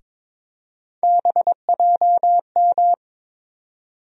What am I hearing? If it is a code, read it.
Morse code, heard as a keyed tone, about 22 wpm, 700 Hz. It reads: BJM
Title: USA Call Sign Postfixes - 3 Letters 22wpm